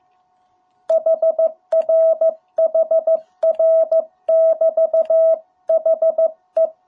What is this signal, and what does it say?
Morse code, heard as a keyed tone, about 15 wpm, 650 Hz. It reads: HR HR <BT> HE